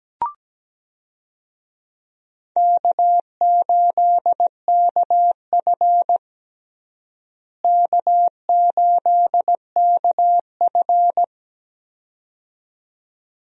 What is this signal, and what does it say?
Morse code, heard as a keyed tone, about 17 wpm, 700 Hz. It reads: K8KF K8KF
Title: USA Call Signs - 4 Characters - Repeated 2x - 17wpm